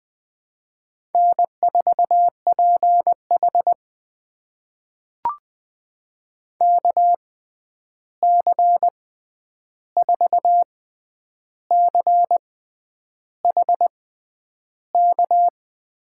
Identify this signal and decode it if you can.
Morse code, heard as a keyed tone, about 20 wpm, spaced as for 9 wpm, 700 Hz. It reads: N4PH KC4CHK
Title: USA Call Signs - 6x Character Spacing - 20wpm